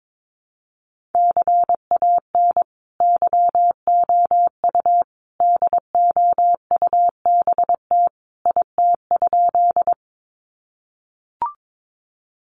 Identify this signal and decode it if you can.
Morse code, heard as a keyed tone, about 22 wpm, 700 Hz. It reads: CAN YOU DOUBT IT?
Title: Sentences from Top 1500 Words 22wpm